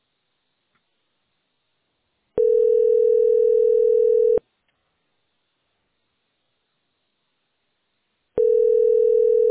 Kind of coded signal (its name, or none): telephone call progress tone